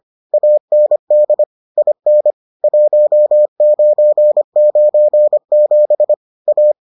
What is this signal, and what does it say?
Morse code, heard as a keyed tone, about 25 wpm, 600 Hz. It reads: AND IN 1997 A